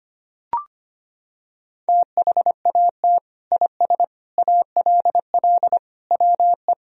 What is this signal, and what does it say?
Morse code, heard as a keyed tone, about 25 wpm, 700 Hz. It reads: THAT IS ALL WE
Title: Sentences from Top 1000 Words - Review Part 2 - 25wpm